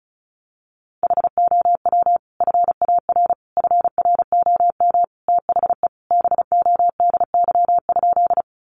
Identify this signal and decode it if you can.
Morse code, heard as a keyed tone, about 34 wpm, 700 Hz. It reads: HOW FAR FROM THE BODY?
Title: Sentences from Top 400 Words 35wpm